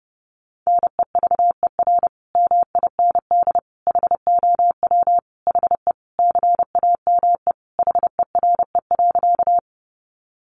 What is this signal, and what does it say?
Morse code, heard as a keyed tone, about 30 wpm, 700 Hz. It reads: NEVER MIND HOW HE CAME HERE.
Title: Sentences from Top 500 Words 30wpm